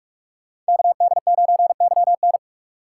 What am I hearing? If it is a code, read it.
Morse code, heard as a keyed tone, about 44 wpm, 700 Hz. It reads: KD9YN